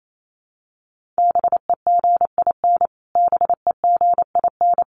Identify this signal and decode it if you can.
Morse code, heard as a keyed tone, about 28 wpm, 700 Hz. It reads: BEGIN BEGIN
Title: Top 300 Words Repeated 28wpm